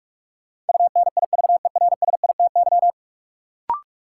Morse code, heard as a keyed tone, about 46 wpm, 700 Hz. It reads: UNIVERSITY
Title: Sentences from Top 1000 Words - Review Part 4 - New Word Drill - 45wpm